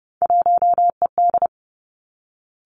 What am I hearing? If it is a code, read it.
Morse code, heard as a keyed tone, about 30 wpm, 700 Hz. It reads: 1ED
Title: Three Challenging Letter-Number Combinations 30wpm